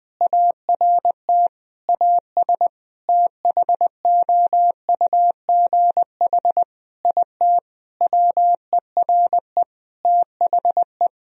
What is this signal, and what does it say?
Morse code, heard as a keyed tone, about 20 wpm, 700 Hz. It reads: ART AS THOUGH IT WERE THE
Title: Sentences from Top 1000 Words - Review Part 3 - 20wpm